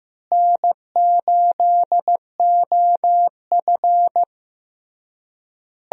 Morse code, heard as a keyed tone, about 15 wpm, 700 Hz. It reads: N8OF
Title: USA Call Signs - 4 Characters - Repeated 3x - 15wpm